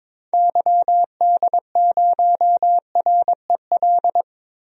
Morse code, heard as a keyed tone, about 22 wpm, 700 Hz. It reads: YD0REL
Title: DXCC-100 Call Signs - 6 Characters - Repeated 2x - 22wpm